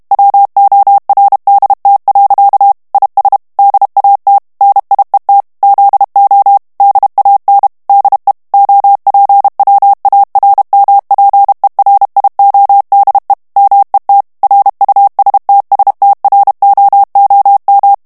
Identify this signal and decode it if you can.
Morse code, heard as a keyed tone, about 32 wpm, 800 Hz. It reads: WORDT. IS DAT NIET ZO DAN DE OPWARMPERIODE MET RUSTSTROOM
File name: HOLLANDAIS texte clair 2 vitesse 30 mots minute